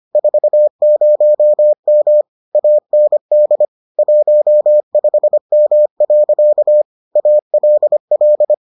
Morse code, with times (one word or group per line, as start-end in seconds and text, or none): 0.2-2.2 40M
2.5-3.7 AND
4.0-6.8 15M.
7.2-8.5 ALL